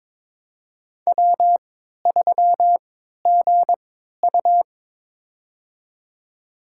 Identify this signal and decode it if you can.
Morse code, heard as a keyed tone, about 21 wpm, spaced as for 12 wpm, 700 Hz. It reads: W3GU
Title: USA Call Signs - 3x Character Spacing - 22wpm